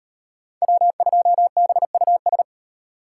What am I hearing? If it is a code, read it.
Morse code, heard as a keyed tone, about 38 wpm, 700 Hz. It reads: W2BUS